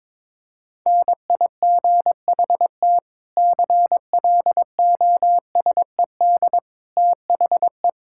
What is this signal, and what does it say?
Morse code, heard as a keyed tone, about 22 wpm, 700 Hz. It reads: NIGHT CLOSED THE
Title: Sentences from Top 1600 Words 22wpm